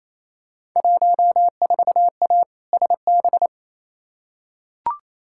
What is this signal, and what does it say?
Morse code, heard as a keyed tone, about 28 wpm, 700 Hz. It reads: 14A SB
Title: Field Day Exchanges Encore - 28wpm